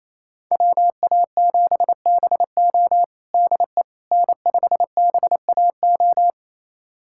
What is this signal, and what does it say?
Morse code, heard as a keyed tone, about 28 wpm, 700 Hz. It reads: WA7BO DE N5BAO
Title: USA Callsign DE Callsign - 5 Characters - 28wpm